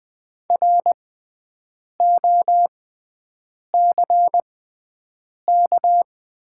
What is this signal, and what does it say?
Morse code, heard as a keyed tone, about 20 wpm, spaced as for 7 wpm, 700 Hz. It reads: ROCK
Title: 6x Character Spacing - Top 500 Words - 20wpm